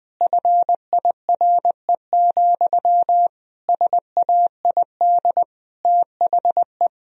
Morse code, heard as a keyed tone, about 20 wpm, 700 Hz. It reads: FIRE, SAID THE
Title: Sentences from Top 1000 Words - Review Part 1 - 20wpm